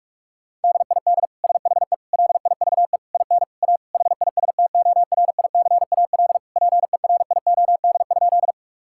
Morse code, heard as a keyed tone, about 46 wpm, 700 Hz. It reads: DID SHE LIVE IN A HISTORICAL PERIOD?
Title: Sentences from Top 1000 Words - Review Part 5 - 45wpm